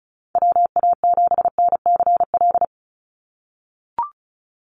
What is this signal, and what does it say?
Morse code, heard as a keyed tone, about 34 wpm, 700 Hz. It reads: WA7NCL
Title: NA SOTA Activator Callsigns 35wpm